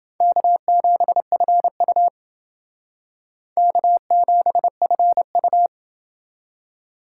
Morse code, heard as a keyed tone, about 28 wpm, 700 Hz. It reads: K7FU K7FU